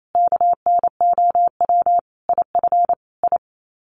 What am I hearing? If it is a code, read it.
Morse code, heard as a keyed tone, about 28 wpm, 700 Hz. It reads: KNOW IF I
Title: Sets of 5 Words 28wpm